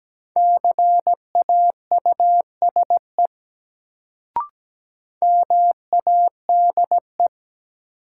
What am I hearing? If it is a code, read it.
Morse code, heard as a keyed tone, about 17 wpm, 700 Hz. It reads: CAUSE MADE
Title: Sentences from Top 200 Words - New Word Drill - 17wpm